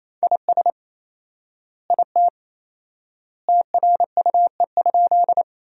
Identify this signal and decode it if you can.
Morse code, heard as a keyed tone, about 28 wpm, 700 Hz. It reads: IS IT TRUE?
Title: 4x Word-Spacing - Sentences from Top 400 Words - 28wpm